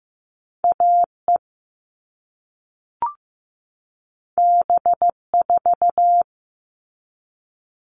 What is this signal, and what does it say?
Morse code, heard as a keyed tone, about 15 wpm, 700 Hz. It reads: AE B4
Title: All Cut Numbers - 2-Digit 15wpm